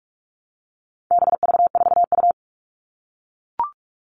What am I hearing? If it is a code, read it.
Morse code, heard as a keyed tone, about 44 wpm, 700 Hz. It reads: BV4U
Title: All Cut Numbers - 4-Digit 45wpm